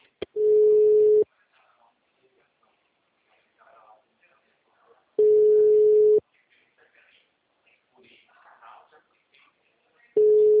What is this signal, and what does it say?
Ringback tone (Europe, 425 Hz), with speech.
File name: out- -2020- - - .8360